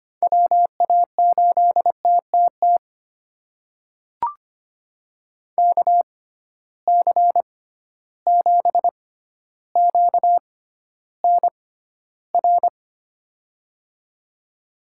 Morse code, heard as a keyed tone, about 25 wpm, spaced as for 13 wpm, 700 Hz. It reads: WA8TTT KC7QNR
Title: USA Call Signs - 6 Characters - 6x Character Spacing - 25wpm